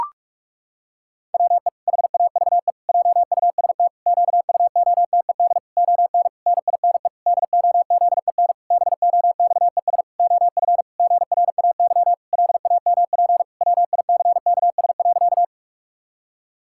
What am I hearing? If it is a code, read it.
Morse code, heard as a keyed tone, about 41 wpm, 700 Hz. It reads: WE HAVE JUST QUOTED ON NINE DOZEN BOXES OF GRAY LAMP WICKS.
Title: Pangram Challenge v2 - 45wpm